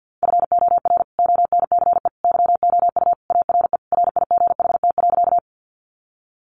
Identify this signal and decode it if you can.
Morse code, heard as a keyed tone, about 50 wpm, 700 Hz. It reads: FOR ONCE YOU ARE RIGHT.
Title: Sentences from Top 300 Words 50wpm